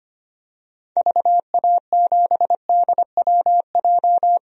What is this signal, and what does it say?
Morse code, heard as a keyed tone, about 25 wpm, 700 Hz. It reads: VA7DWJ
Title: DXCC-100 Call Signs - 6 Characters - 25wpm